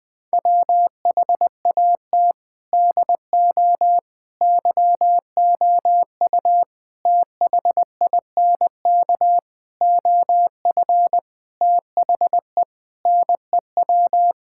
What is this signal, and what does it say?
Morse code, heard as a keyed tone, about 20 wpm, 700 Hz. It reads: WHAT DO YOU THINK OF THE NEW
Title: Sentences with 6-Word Phrases - 20wpm